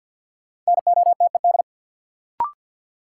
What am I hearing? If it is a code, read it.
Morse code, heard as a keyed tone, about 52 wpm, 700 Hz. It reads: NOTED
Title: Sets of 5 Words - New Word Drill - 50wpm